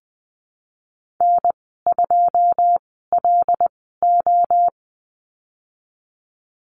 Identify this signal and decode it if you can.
Morse code, heard as a keyed tone, about 20 wpm, spaced as for 14 wpm, 700 Hz. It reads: N2LO
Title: 167 Lesson 27g - Number 7 - Call Signs 20wpm